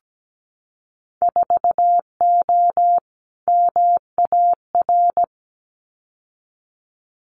Morse code, heard as a keyed tone, about 17 wpm, 700 Hz. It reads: 4O MAR
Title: Winter Field Day Exchanges Encore 17wpm